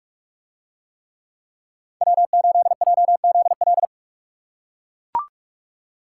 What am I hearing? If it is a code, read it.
Morse code, heard as a keyed tone, about 44 wpm, 700 Hz. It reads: W8JZL